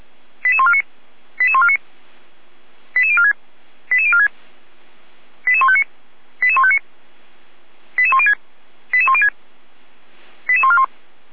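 Five-tone selective call (ZVEI), heard as quick runs of five tones, tones of about 70 ms. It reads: 88149 88149 88046 88046 88159 88159 88106 88106 88141
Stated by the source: ZVEI five-tone selective call 88194 received by SDR, archived by Feuerwehr Orlamünde (fire brigade).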